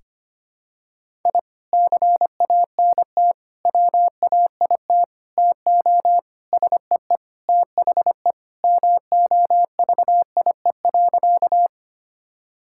Morse code, heard as a keyed tone, about 25 wpm, 700 Hz. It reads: I CANT WAIT TO SEE THE MOVIE.
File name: practice / 25wpm / Sentences with 3-Word Phrases - 25wpm